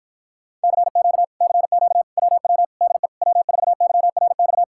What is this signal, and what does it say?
Morse code, heard as a keyed tone, about 53 wpm, 700 Hz. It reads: CQ CQ WW DE W4YRX